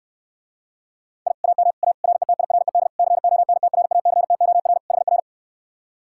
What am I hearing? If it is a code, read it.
Morse code, heard as a keyed tone, about 69 wpm, 700 Hz. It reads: I AM A RETIRED CONTRACTOR HR